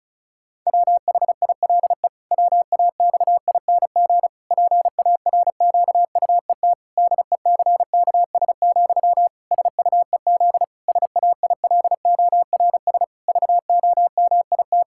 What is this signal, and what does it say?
Morse code, heard as a keyed tone, about 34 wpm, 700 Hz. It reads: WHILE WAXING PARQUET DECKS, SUEZ SAILORS VOMIT
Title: Pangram Challenge v2 - 35wpm